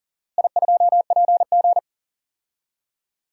Morse code, heard as a keyed tone, about 40 wpm, 700 Hz. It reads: I2PG
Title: DXCC-100 Call Signs - 4 Characters - Repeated 3x - 40wpm